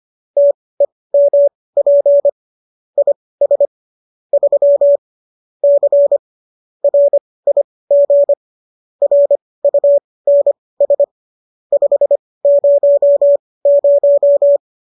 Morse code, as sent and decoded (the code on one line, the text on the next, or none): - . -- .--. / .. ... / ...-- / -.-. / .-. .. --. / .-. ..- -. ... / ..... ----- -----
TEMP IS 3 C RIG RUNS 500